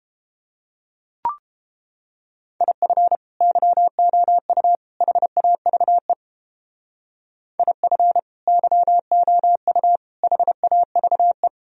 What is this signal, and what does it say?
Morse code, heard as a keyed tone, about 31 wpm, 700 Hz. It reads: IF YOU HAVE IF YOU HAVE